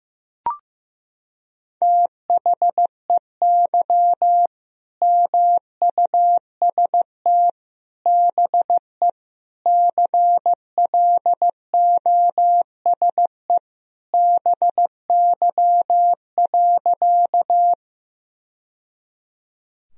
Morse code, heard as a keyed tone, about 15 wpm, 700 Hz. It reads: THEY MUST BE CLOSE BY.
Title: Sentences from Top 1000 Words - Review Part 1 - 15wpm